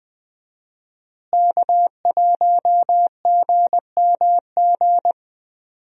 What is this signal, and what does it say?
Morse code, heard as a keyed tone, about 21 wpm, 700 Hz. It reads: K1GMG
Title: USA Call Signs - 5 Characters - Repeated 2x - 20wpm